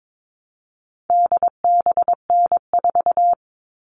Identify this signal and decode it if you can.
Morse code, heard as a keyed tone, about 22 wpm, 700 Hz. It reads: DBN4
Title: All Cut Numbers - 4-Digit 22wpm